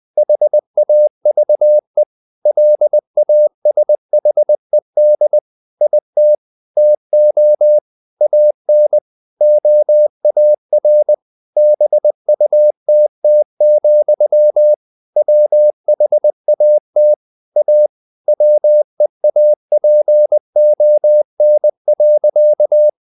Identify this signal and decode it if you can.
Morse code, heard as a keyed tone, about 20 wpm, 600 Hz. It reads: HAVE LASHED IT TO AN OAR BUTT, WHAT A WEAPON.
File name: the old man and the sea part 2 20wpm